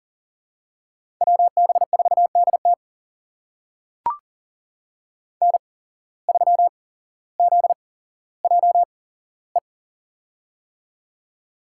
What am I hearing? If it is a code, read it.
Morse code, heard as a keyed tone, about 40 wpm, spaced as for 17 wpm, 700 Hz. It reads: WB4DT N3ZJE
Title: USA Call Signs - 8x Character Spacing - 40wpm